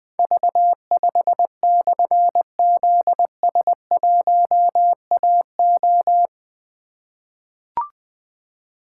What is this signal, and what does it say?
Morse code, heard as a keyed tone, about 21 wpm, 700 Hz. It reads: V5/ZS1AO
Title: Difficult Call Signs 20wpm